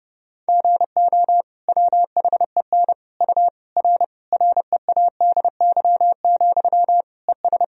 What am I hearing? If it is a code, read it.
Morse code, heard as a keyed tone, about 30 wpm, 700 Hz. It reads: GO WHEN U R READY, ES
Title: Sentences from Top 500 Words and Abbreviations 30wpm